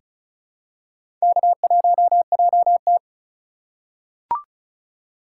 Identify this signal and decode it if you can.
Morse code, heard as a keyed tone, about 34 wpm, 700 Hz. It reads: K1JT
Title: USA Call Signs - 2x Character Spacing - 35wpm